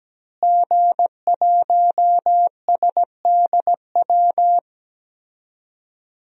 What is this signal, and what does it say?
Morse code, heard as a keyed tone, about 17 wpm, 700 Hz. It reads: G1SDW
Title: DXCC-100 Call Signs - 5 Characters - Repeated 3x - 17wpm